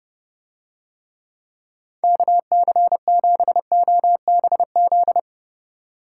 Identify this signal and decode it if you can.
Morse code, heard as a keyed tone, about 31 wpm, 700 Hz. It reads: KC7OBZ